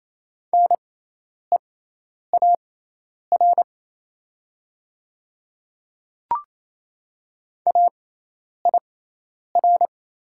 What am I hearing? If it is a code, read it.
Morse code, heard as a keyed tone, about 28 wpm, spaced as for 10 wpm, 690 Hz. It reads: NEAR AIR
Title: 6x Character Spacing - Top 300 Words - 28wpm